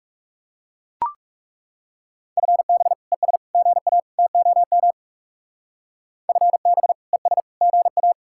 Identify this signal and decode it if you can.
Morse code, heard as a keyed tone, about 37 wpm, 700 Hz. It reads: FB ES GA TOM FB ES GA
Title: Speed-Racing Common QSO Phrases Encore - 30wpm